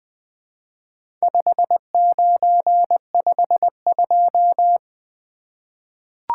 Morse code, heard as a keyed tone, about 21 wpm, 700 Hz. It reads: 5952
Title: Four Letter-Number Combinations - 20wpm